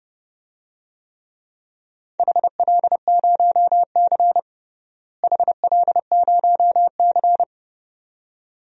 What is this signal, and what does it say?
Morse code, heard as a keyed tone, about 30 wpm, 700 Hz. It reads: HL0C HL0C
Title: DXCC-100 Call Signs - 4 Characters - Repeated 2x - 30wpm